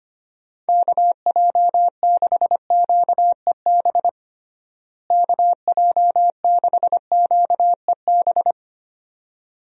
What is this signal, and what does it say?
Morse code, heard as a keyed tone, about 25 wpm, 700 Hz. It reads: KJ6QEB KJ6QEB
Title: USA Call Signs - 6 Characters - Repeated 2x - 25wpm